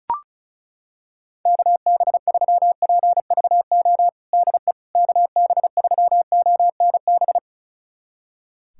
Morse code, heard as a keyed tone, about 34 wpm, 690 Hz. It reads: KB3PVO DE KB3ONB
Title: USA Callsign DE Callsign - 6 Characters - 35wpm